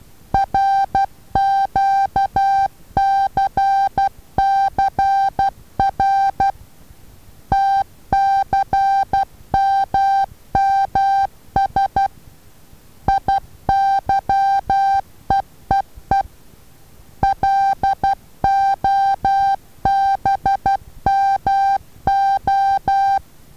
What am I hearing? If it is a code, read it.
Morse code, heard as a keyed tone, about 12 wpm, 790 Hz. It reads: RQCCR TCMMS IYEEE LOBMO
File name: Taviro HA6VA B